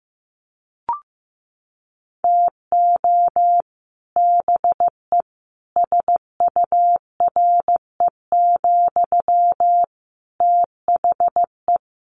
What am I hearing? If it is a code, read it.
Morse code, heard as a keyed tone, about 15 wpm, 700 Hz. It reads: TO BE SURE, THE
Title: Sentences from Top 1000 Words 15wpm